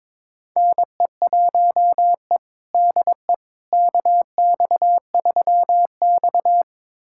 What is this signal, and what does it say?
Morse code, heard as a keyed tone, about 22 wpm, 700 Hz. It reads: NE1E DE KX3X
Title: USA Callsign DE Callsign - 4 Characters - 22wpm